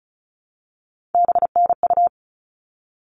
Morse code, heard as a keyed tone, about 34 wpm, 700 Hz. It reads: BNU
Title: All Cut Numbers - 3-Digit 35wpm